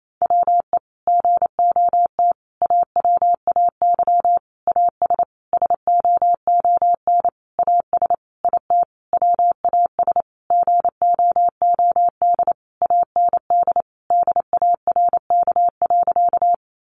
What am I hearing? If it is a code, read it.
Morse code, heard as a keyed tone, about 28 wpm, 700 Hz. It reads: WE GOT AWAY AS SOON AS IT WAS GOOD AND DARK.
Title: Sentences from Top 600 Words 28wpm